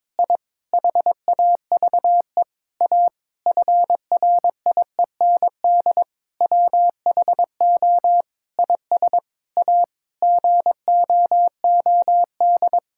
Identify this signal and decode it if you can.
Morse code, heard as a keyed tone, about 22 wpm, 700 Hz. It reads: I HAVE A FRIEND WHO IS A GOOD
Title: Sentences from Top 1000 Words - Review Part 5 - 22wpm